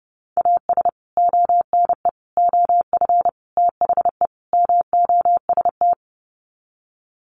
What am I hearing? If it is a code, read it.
Morse code, heard as a keyed tone, about 30 wpm, 700 Hz. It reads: AS ONE OF THE MOST
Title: Sets of 5 Words 30wpm